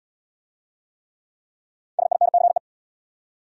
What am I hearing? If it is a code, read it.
Morse code, heard as a keyed tone, about 70 wpm, 700 Hz. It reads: SINCE